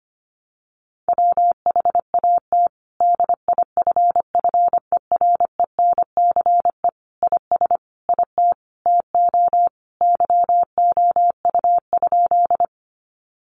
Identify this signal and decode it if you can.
Morse code, heard as a keyed tone, about 25 wpm, 700 Hz. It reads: WHAT DIFFERENCE IS IT TO YOU?
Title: Sentences from Top 800 Words 25wpm